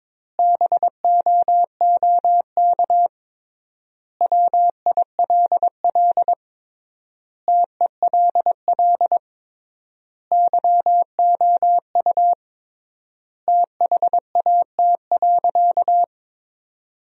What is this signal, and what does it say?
Morse code, heard as a keyed tone, about 22 wpm, 700 Hz. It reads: BOOK WILL TELL YOU THAT.
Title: 3x Word-Spacing - Sentences from Top 300 Words - 22wpm